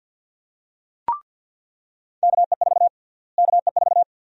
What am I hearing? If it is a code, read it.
Morse code, heard as a keyed tone, about 46 wpm, 700 Hz. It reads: KE4 KE4
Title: USA Call Sign Prefixes - Repeated 2x - 50wpm